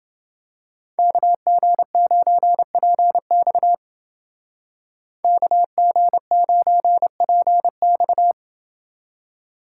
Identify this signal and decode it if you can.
Morse code, heard as a keyed tone, about 28 wpm, 700 Hz. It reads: KG9PX KG9PX